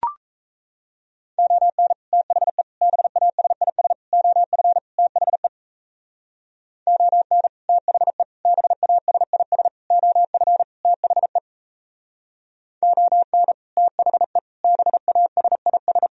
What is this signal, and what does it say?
Morse code, heard as a keyed tone, about 35 wpm, 700 Hz. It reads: ON THE BASIS OF THE ON THE BASIS OF THE ON THE BASIS
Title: Speed-Racing - Sets of 5 Words 28wpm